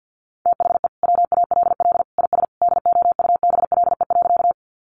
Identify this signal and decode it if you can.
Morse code, heard as a keyed tone, about 50 wpm, 700 Hz. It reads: THE WALL IS DOUBLE.
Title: Sentences from Top 1400 Words 50wpm